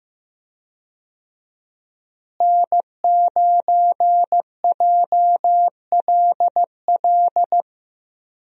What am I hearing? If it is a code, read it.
Morse code, heard as a keyed tone, about 15 wpm, 700 Hz. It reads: N9JLL
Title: USA Call Signs - 5 Characters - Repeated 3x - 15wpm